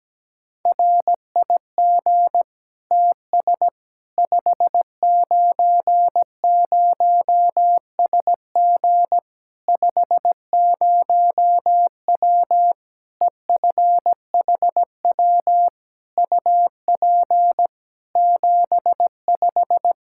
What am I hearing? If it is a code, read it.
Morse code, heard as a keyed tone, about 17 wpm, 700 Hz. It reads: RIG TS 590SG 50W EFHW UP 75
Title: Practice QSO - Exchange 2 - Info - 17wpm